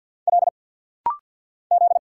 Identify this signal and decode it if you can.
Morse code, heard as a keyed tone, about 50 wpm, 690 Hz. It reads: L Z
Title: Single Letters - Rapid-Fire 50wpm